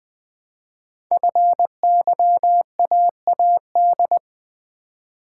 Morse code, heard as a keyed tone, about 20 wpm, 700 Hz. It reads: FYAAD
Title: Code Groups - 5 Letters - 20wpm